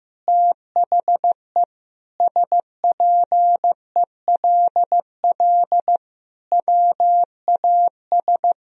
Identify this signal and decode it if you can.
Morse code, heard as a keyed tone, about 15 wpm, 700 Hz. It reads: THE SPELL WAS
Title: Sentences from Top 1000 Words - Review Part 1 - 15wpm